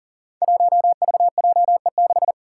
Morse code, heard as a keyed tone, about 40 wpm, 700 Hz. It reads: 1VJE6
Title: Five Challenging Letter-Number Combinations 40wpm